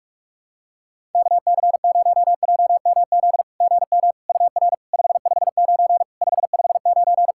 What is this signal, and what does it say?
Morse code, heard as a keyed tone, about 46 wpm, 690 Hz. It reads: KC0JMZ GM UR 559 559
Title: SOTA QSO Practice 45wpm